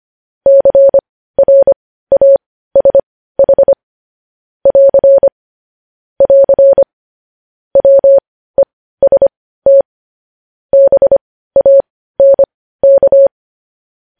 Morse code, heard as a keyed tone, about 25 wpm, spaced as for 15 wpm, 550 Hz. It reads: CRASH <AR> <AR> WEST BANK
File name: podcast / News headlines-1x-25x15